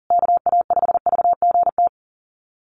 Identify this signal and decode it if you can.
Morse code, heard as a keyed tone, about 37 wpm, 700 Hz. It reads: KA5VGT
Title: USA Call Signs Repeated 3x 40wpm